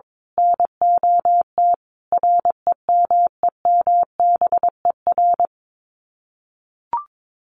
Morse code, heard as a keyed tone, about 22 wpm, 700 Hz. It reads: NOT REMEMBER
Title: Sets of 4 Words 22wpm